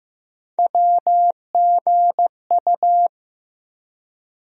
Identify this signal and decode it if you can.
Morse code, heard as a keyed tone, about 15 wpm, 700 Hz. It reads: WGU
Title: USA Call Sign Postfixes - 3 Letters 15wpm